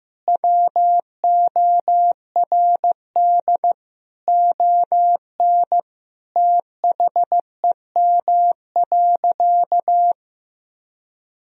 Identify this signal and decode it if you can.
Morse code, heard as a keyed tone, about 16 wpm, 700 Hz. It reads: WORD ON THEM.
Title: Sentences from Top 1000 Words - Review Part 4 - 15wpm